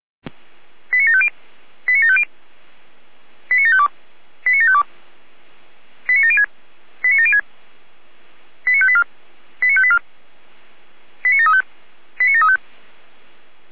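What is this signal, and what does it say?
Five-tone selective call (ZVEI), heard as quick runs of five tones, tones of about 70 ms. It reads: 89750 89750 89752 89752 89706 89706 89584 89584 89735 89735
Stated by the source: ZVEI five-tone selective call received on an SDR.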